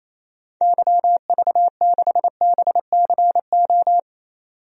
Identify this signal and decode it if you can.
Morse code, heard as a keyed tone, about 28 wpm, 700 Hz. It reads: YV6BCO
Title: DXCC-100 Call Signs - 6 Characters - 28wpm